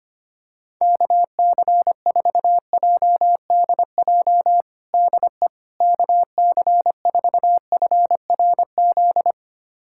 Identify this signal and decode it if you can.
Morse code, heard as a keyed tone, about 25 wpm, 700 Hz. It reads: KC4JDJ DE KC4FRZ
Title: USA Callsign DE Callsign - 6 Characters - 25wpm